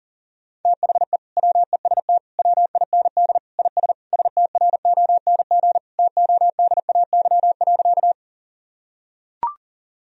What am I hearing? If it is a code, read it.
Morse code, heard as a keyed tone, about 37 wpm, 700 Hz. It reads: THE WEST WIND IS STRONG TODAY.
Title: Sentences from Top 1000 Words - Review Part 2 - 40wpm